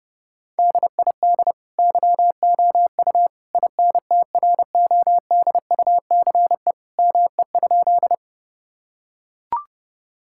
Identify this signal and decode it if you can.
Morse code, heard as a keyed tone, about 30 wpm, 700 Hz. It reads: DID YOU INTRODUCE ME?
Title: Sentences from Top 1600 Words 30wpm